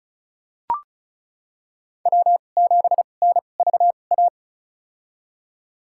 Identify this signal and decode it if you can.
Morse code, heard as a keyed tone, about 36 wpm, spaced as for 25 wpm, 700 Hz. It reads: W7NVA
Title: USA Call Signs - 5 Characters - 2x Character Spacing - 35wpm